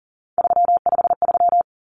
Morse code, heard as a keyed tone, about 40 wpm, 700 Hz. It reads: 353
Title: Common Cut Numbers - 3-Digit 40wpm